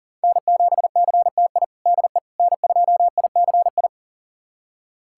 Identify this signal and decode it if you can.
Morse code, heard as a keyed tone, about 40 wpm, 700 Hz. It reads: N7CTI DE N2ICI
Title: USA Callsign DE Callsign - 5 Characters - 40wpm